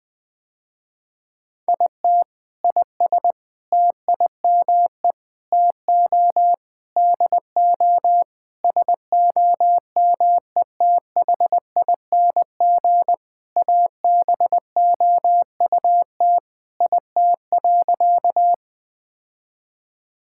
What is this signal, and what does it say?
Morse code, heard as a keyed tone, about 20 wpm, 700 Hz. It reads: IT IS TIME TO DO SOMETHING ABOUT IT.
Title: Sentences with 5-Word Phrases - Spoken Beginning - 20wpm